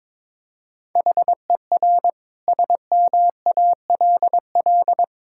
Morse code, heard as a keyed tone, about 22 wpm, 700 Hz. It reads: HER SMALL
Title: Sentences from Top 1000 Words - Review Part 6 - 22wpm